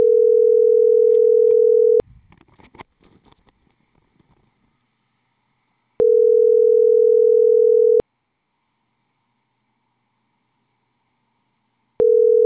Ringback tone (North America), 440 + 480 Hz.